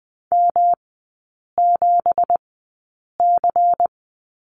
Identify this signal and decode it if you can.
Morse code, heard as a keyed tone, about 20 wpm, spaced as for 8 wpm, 690 Hz. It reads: M7C
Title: 212 Lesson 34e - Number 8 - Three Characters 20wpm